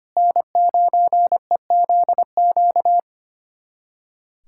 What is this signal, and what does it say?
Morse code, heard as a keyed tone, about 25 wpm, 700 Hz. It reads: N9EZQ
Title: USA Callsign DE Callsign - 5 Characters - 25wpm